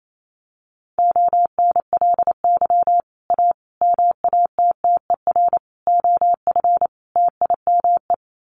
Morse code, heard as a keyed tone, about 28 wpm, 700 Hz. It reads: ONLY A MATTER OF TIME
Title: Sets of 5 Words Encore 28wpm